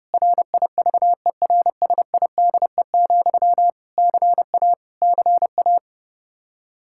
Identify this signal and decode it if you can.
Morse code, heard as a keyed tone, about 30 wpm, 700 Hz. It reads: RIVERSIDE, CA CA
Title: Practice QSO - Exchange 1 - Beginning - 30wpm